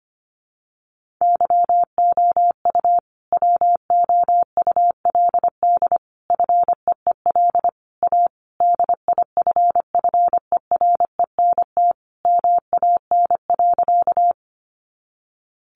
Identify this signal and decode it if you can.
Morse code, heard as a keyed tone, about 25 wpm, 700 Hz. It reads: YOU WOULD FEEL A DIFFERENT MAN.
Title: Sentences from Top 600 Words 25wpm